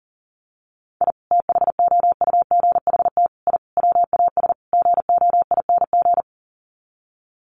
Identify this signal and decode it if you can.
Morse code, heard as a keyed tone, about 40 wpm, 700 Hz. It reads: I THOUGHT I WAS GOING
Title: Sets of 5 Words Encore2 40wpm